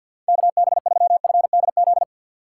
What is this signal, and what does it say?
Morse code, heard as a keyed tone, about 52 wpm, 700 Hz. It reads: KB3FDZ
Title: USA Call Signs - 6 Characters - 8x Character Spacing - 50wpm